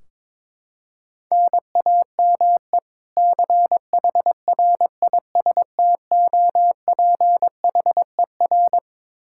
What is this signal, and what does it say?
Morse code, heard as a keyed tone, about 22 wpm, 700 Hz. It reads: NAME CHRISTOPHER
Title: Practice QSO - Exchange 1 - Beginning - 22wpm